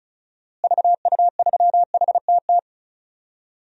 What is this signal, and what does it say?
Morse code, heard as a keyed tone, about 34 wpm, 700 Hz. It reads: VU3HTT
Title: DXCC-100 Call Signs - 6 Characters - Repeated 2x - 35wpm